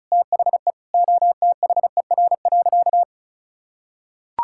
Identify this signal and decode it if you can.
Morse code, heard as a keyed tone, about 36 wpm, 700 Hz. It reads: THE OTHER.
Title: Sentences from Top 1000 Words - Review Part 6 - 35wpm